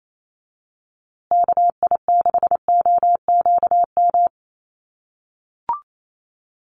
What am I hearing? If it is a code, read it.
Morse code, heard as a keyed tone, about 28 wpm, 700 Hz. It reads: KI6OQM
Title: USA Call Signs Repeated 3x 28wpm